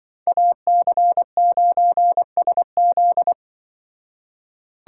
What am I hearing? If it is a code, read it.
Morse code, heard as a keyed tone, about 25 wpm, 700 Hz. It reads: AC9SZ
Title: Speed Racing - USA 2x2 Call Signs 20wpm